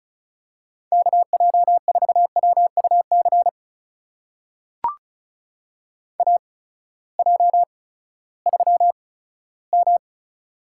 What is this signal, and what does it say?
Morse code, heard as a keyed tone, about 36 wpm, spaced as for 18 wpm, 700 Hz. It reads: KJ4WUC AJ3M
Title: USA Call Signs - 8x Character Spacing - 35wpm